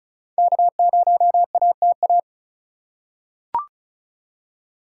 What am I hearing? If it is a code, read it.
Morse code, heard as a keyed tone, about 34 wpm, 700 Hz. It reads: K0ATA
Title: USA Call Signs - 5 Characters - 4x Character Spacing - 35wpm